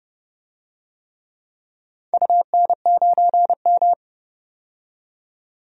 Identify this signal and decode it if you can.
Morse code, heard as a keyed tone, about 30 wpm, 700 Hz. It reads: UN9M
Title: DXCC-100 Call Signs - 4 Characters - 30wpm